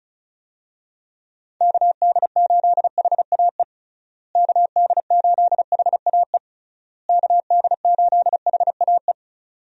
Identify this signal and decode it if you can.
Morse code, heard as a keyed tone, about 36 wpm, 700 Hz. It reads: KD8HAE KD8HAE KD8HAE
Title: USA Call Signs - 6 Characters - Repeated 3x - 35wpm